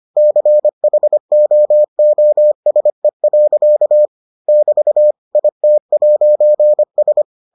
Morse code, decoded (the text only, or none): CHOOSE. <BT> IT'S